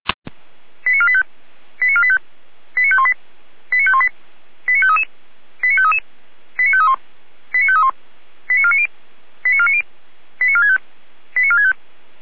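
Five-tone selective call (ZVEI), heard as quick runs of five tones, tones of about 70 ms. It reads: 89485 89485 89518 89518 89533 89533 89531 89531 89490 89490 89465 89465
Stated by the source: ZVEI five-tone selective call received with an SDR.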